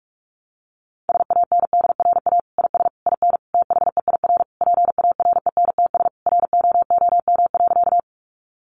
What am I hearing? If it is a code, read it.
Morse code, heard as a keyed tone, about 44 wpm, 700 Hz. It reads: SANDRA IS IN THEIR PARENTS ROOM.
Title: Easy Phrases 45wpm